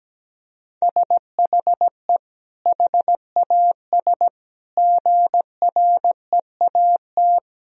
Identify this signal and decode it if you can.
Morse code, heard as a keyed tone, about 17 wpm, 700 Hz. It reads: SHE HAS GREAT
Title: Sentences from Top 1000 Words - Review Part 4 - 17wpm